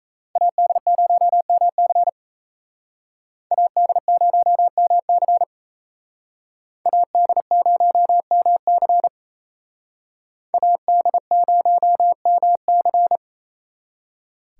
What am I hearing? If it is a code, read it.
Morse code, heard as a keyed tone, about 35 wpm, 700 Hz. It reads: AD0MC AD0MC AD0MC AD0MC
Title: Speed Racing - USA 2x2 Call Signs 28wpm